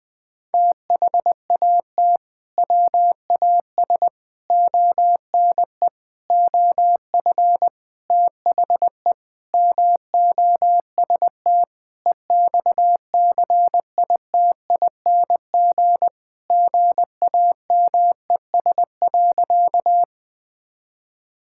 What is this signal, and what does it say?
Morse code, heard as a keyed tone, about 21 wpm, 700 Hz. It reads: THAT WAS ONE OF THE MOST EXCITING GAMES.
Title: Sentences with 5-Word Phrases - Spoken Beginning - 20wpm